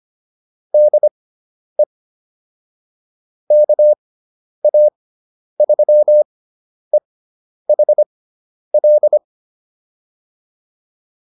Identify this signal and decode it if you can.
Morse code, heard as a keyed tone, about 25 wpm, spaced as for 10 wpm, 600 Hz. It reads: DE KA3EHL
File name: POTA109 10